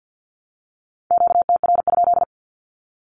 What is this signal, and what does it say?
Morse code, heard as a keyed tone, about 50 wpm, 700 Hz. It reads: QTR?
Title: Q-Codes Repeated 3x 50wpm